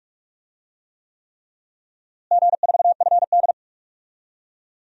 Morse code, heard as a keyed tone, about 42 wpm, 700 Hz. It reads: G4FD
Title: DXCC-100 Call Signs - 4 Characters - Repeated 3x - 45wpm